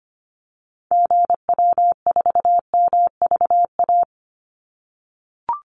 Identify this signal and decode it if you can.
Morse code, heard as a keyed tone, about 25 wpm, 700 Hz. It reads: GW4MVA
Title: Call Signs - 25wpm